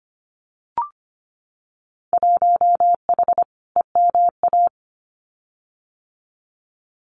Morse code, heard as a keyed tone, about 25 wpm, 700 Hz. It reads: 1H EMA
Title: Winter Field Day Exchanges 25wpm